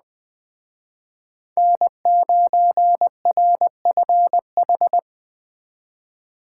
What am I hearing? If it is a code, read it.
Morse code, heard as a keyed tone, about 20 wpm, 700 Hz. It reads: N9RFH
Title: Speed Racing - USA Call Signs 17wpm